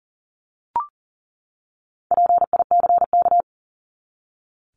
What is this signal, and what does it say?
Morse code, heard as a keyed tone, about 40 wpm, 700 Hz. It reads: PICK
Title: Top 751-1000 Words 40wpm